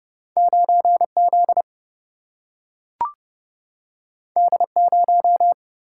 Morse code, heard as a keyed tone, about 30 wpm, 700 Hz. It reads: 9Z D0
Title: Single-Letter Single-Number 30wpm